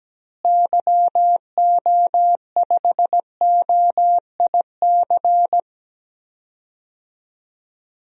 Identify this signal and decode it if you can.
Morse code, heard as a keyed tone, about 17 wpm, 700 Hz. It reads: YO5OIC
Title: DXCC-100 Call Signs - 6 Characters - Repeated 2x - 17wpm